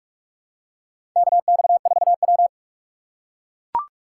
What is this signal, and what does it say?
Morse code, heard as a keyed tone, about 42 wpm, 700 Hz. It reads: KX4W
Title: USA Call Signs - 4 Characters - 2x Character Spacing - 45wpm